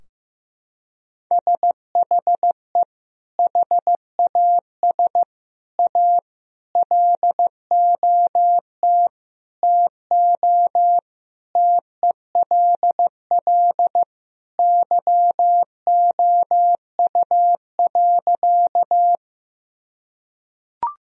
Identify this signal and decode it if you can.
Morse code, heard as a keyed tone, about 15 wpm, 700 Hz. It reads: SHE HAS A LOT TO TELL YOU.
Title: Sentences from Top 1000 Words - Review Part 3 - 15wpm